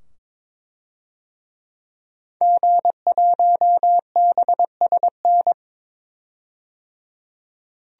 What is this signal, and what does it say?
Morse code, heard as a keyed tone, about 22 wpm, 700 Hz. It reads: G1BSN
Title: DXCC-100 Call Signs - 5 Characters - 22wpm